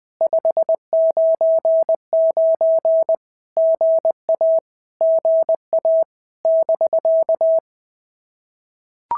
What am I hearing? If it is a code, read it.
Morse code, heard as a keyed tone, about 20 wpm, 650 Hz. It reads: 599 GA GA <BK>
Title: POTA QSO Practice - 20wpm